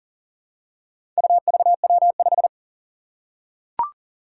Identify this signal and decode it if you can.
Morse code, heard as a keyed tone, about 40 wpm, 700 Hz. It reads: UVW5
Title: Four Challenging Letter-Number Combinations 40wpm